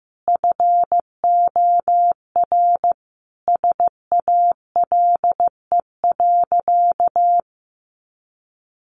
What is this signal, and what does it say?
Morse code, heard as a keyed tone, about 15 wpm, 700 Hz. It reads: FOR SALE.
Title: Sentences from Top 1400 Words 15wpm